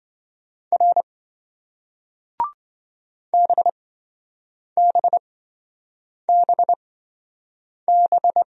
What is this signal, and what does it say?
Morse code, heard as a keyed tone, about 27 wpm, spaced as for 7 wpm, 700 Hz. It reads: R BBBB